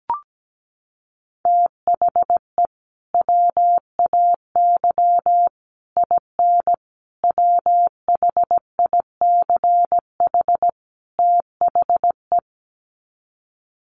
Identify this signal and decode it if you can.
Morse code, heard as a keyed tone, about 17 wpm, 700 Hz. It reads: THE WAY IN WHICH THE
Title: Sets of 5 Words Encore2 17wpm